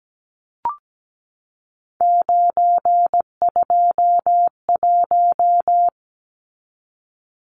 Morse code, heard as a keyed tone, about 17 wpm, 700 Hz. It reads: 921
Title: Numbers - 3-Digit 17wpm